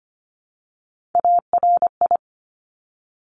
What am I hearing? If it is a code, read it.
Morse code, heard as a keyed tone, about 25 wpm, 700 Hz. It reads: ARI
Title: Top Three Letter Combinations in Words 25wpm